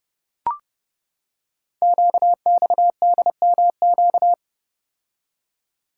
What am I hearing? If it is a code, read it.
Morse code, heard as a keyed tone, about 30 wpm, 700 Hz. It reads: QXDMQ
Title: Code Groups - 5 Letters - 30wpm